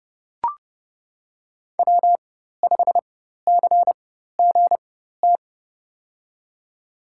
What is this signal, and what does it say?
Morse code, heard as a keyed tone, about 30 wpm, spaced as for 14 wpm, 700 Hz. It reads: W5CGT
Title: USA Call Signs - 5 Characters - 4x Character Spacing - 30wpm